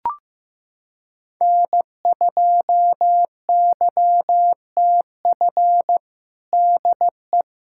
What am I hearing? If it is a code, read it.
Morse code, heard as a keyed tone, about 16 wpm, 700 Hz. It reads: N2YTF DE
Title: USA Callsign DE Callsign - 5 Characters - 15wpm